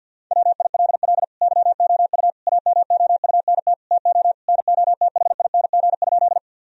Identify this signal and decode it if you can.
Morse code, heard as a keyed tone, about 50 wpm, 700 Hz. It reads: WILL YOU AMOUNT TO NOTHING?
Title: Sentences from Top 1000 Words - Review Part 4 - 50wpm